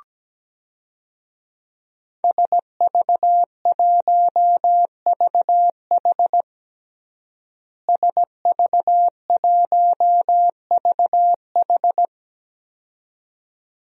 Morse code, heard as a keyed tone, about 17 wpm, 700 Hz. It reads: SV1VH SV1VH
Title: DXCC-100 Call Signs - 5 Characters - Repeated 2x - 17wpm